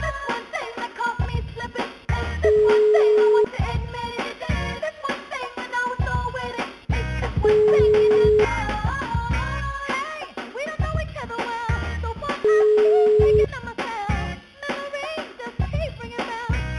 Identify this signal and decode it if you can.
Ringback tone (Europe, 425 Hz), with noise.